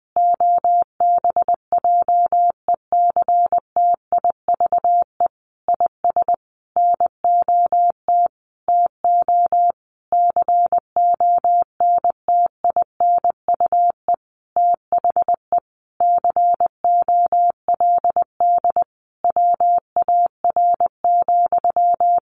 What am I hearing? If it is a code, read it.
Morse code, heard as a keyed tone, about 20 wpm, 700 Hz. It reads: OBJECTIVE IS NOT TO CONTINUE THE COLD WAR,
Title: Sentences from Top 1200 Words 20wpm